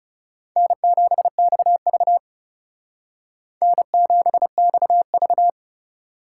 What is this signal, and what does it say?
Morse code, heard as a keyed tone, about 36 wpm, 700 Hz. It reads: N7XV N7XV
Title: Speed Racing - USA Call Signs 30wpm